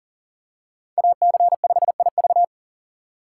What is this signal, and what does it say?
Morse code, heard as a keyed tone, about 37 wpm, 700 Hz. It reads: AC5IV
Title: USA Call Signs - 5 Characters - 3x Character Spacing - 40wpm